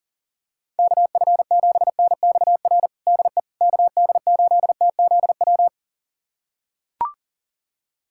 Morse code, heard as a keyed tone, about 40 wpm, 700 Hz. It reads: KF7NXR DE KD8TZW
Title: USA Callsign DE Callsign - 6 Characters - 40wpm